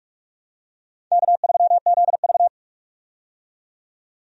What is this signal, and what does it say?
Morse code, heard as a keyed tone, about 46 wpm, 700 Hz. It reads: K3ZV
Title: USA Call Signs - 4 Characters - 45wpm